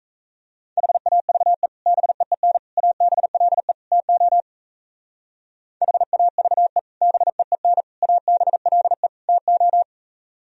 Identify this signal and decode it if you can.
Morse code, heard as a keyed tone, about 37 wpm, 700 Hz. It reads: HAVE BEEN ABLE TO HAVE BEEN ABLE TO